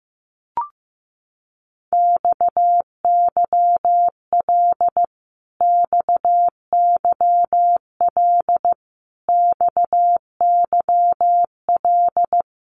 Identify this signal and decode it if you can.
Morse code, heard as a keyed tone, about 15 wpm, 700 Hz. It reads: XYL XYL XYL
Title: Abbreviations Repeated 3x - 15wpm